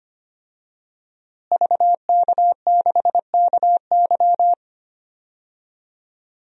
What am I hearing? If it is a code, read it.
Morse code, heard as a keyed tone, about 25 wpm, 700 Hz. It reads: VK6KY
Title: DXCC-100 Call Signs - 5 Characters - Repeated 2x - 25wpm